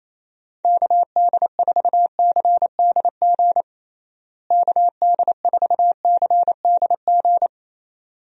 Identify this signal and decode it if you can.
Morse code, heard as a keyed tone, about 28 wpm, 700 Hz. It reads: KD4CDG KD4CDG
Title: USA Call Signs - 6 Characters - Repeated 2x - 28wpm